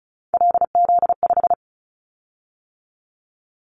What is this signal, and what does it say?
Morse code, heard as a keyed tone, about 36 wpm, 700 Hz. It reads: LZ5
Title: Callsign Prefixes for Top 100 DXCC - 35wpm